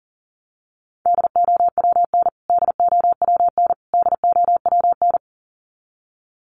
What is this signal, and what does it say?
Morse code, heard as a keyed tone, about 40 wpm, 700 Hz. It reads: DOWN DOWN DOWN
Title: Top 300 Words Repeated 40wpm